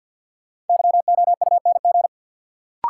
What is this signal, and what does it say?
Morse code, heard as a keyed tone, about 52 wpm, 700 Hz. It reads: YOUNG
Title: Sentences from Top 400 Words - New Word Drill - 50wpm